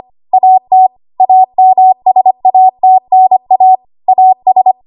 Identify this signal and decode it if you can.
Morse code, heard as a keyed tone, about 25 wpm, 750 Hz. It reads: AT AMSATNA AS
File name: Archive / 25 WPM / 25WPM